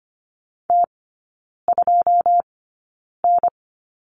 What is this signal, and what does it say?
Morse code, heard as a keyed tone, about 25 wpm, spaced as for 9 wpm, 700 Hz. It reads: T2N
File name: Lesson 14e - Number 2 - Three Characters 25wpm